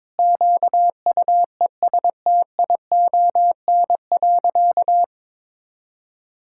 Morse code, heard as a keyed tone, about 22 wpm, 700 Hz. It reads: QUESTION.
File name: Sentences with 7-Word Phrases - 22wpm